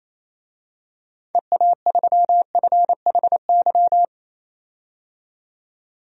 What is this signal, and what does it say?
Morse code, heard as a keyed tone, about 28 wpm, 700 Hz. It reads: EA3FHY
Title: DXCC-100 Call Signs - 6 Characters - 28wpm